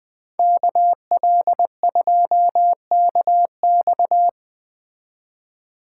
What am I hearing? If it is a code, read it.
Morse code, heard as a keyed tone, about 20 wpm, 700 Hz. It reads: KL2KX